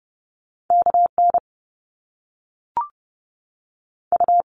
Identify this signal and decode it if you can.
Morse code, heard as a keyed tone, about 30 wpm, 700 Hz. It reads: KN U
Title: Top Two Letter Combinations in Words 30wpm